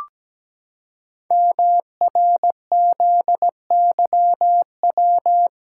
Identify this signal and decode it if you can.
Morse code, heard as a keyed tone, about 17 wpm, 700 Hz. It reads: MRZYW
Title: Code Groups - 5 Letters - 17wpm